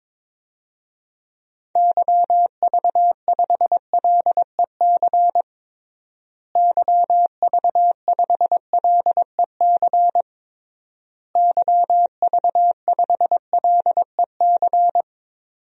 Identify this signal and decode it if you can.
Morse code, heard as a keyed tone, about 22 wpm, 700 Hz. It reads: YV5LEC YV5LEC YV5LEC
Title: DXCC-100 Call Signs - 6 Characters - Repeated 3x - 22wpm